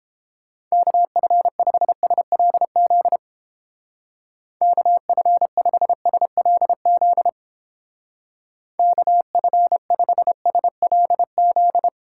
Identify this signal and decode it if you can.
Morse code, heard as a keyed tone, about 29 wpm, 700 Hz. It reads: KF5SLZ KF5SLZ KF5SLZ